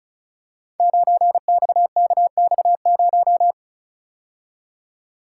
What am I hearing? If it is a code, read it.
Morse code, heard as a keyed tone, about 36 wpm, 700 Hz. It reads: 9XKX0
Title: Code Groups - 5 Letter:Numbers - 35wpm